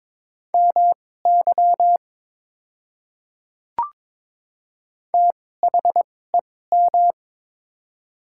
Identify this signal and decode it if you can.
Morse code, heard as a keyed tone, about 22 wpm, spaced as for 16 wpm, 700 Hz. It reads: MY THEM
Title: 2x Character Spacing - Top 100 Words - 22wpm